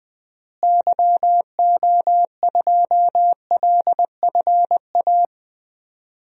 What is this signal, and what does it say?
Morse code, heard as a keyed tone, about 21 wpm, 700 Hz. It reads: YO2LFA
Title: DXCC-100 Call Signs - 6 Characters - Repeated 2x - 20wpm